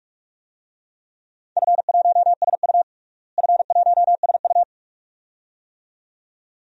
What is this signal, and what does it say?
Morse code, heard as a keyed tone, about 41 wpm, 700 Hz. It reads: F1SU F1SU
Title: DXCC-100 Call Signs - 4 Characters - Repeated 2x - 45wpm